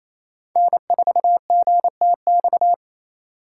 Morse code, heard as a keyed tone, about 35 wpm, 700 Hz. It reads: N4GTX
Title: Speed Racing - USA 1x3 Call Signs 28wpm